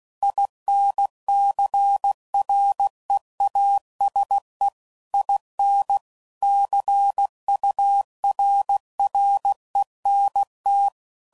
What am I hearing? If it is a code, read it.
Morse code, heard as a keyed tone, about 16 wpm, 800 Hz. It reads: INCREASE IN CURRENT
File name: CW Code Groups @15WPM / Technical Subjects / XL